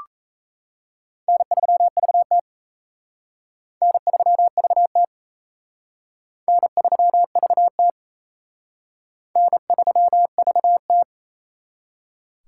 Morse code, heard as a keyed tone, about 34 wpm, 700 Hz. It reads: N3VT N3VT N3VT N3VT
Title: Speed Racing - USA 1x2 Call Signs 28wpm